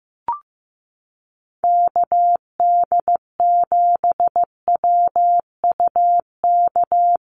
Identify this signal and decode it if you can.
Morse code, heard as a keyed tone, about 15 wpm, 700 Hz. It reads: KD7WUK
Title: USA Call Signs Repeated 3x 15wpm